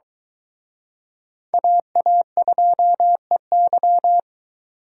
Morse code, heard as a keyed tone, about 22 wpm, 700 Hz. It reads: AA2EY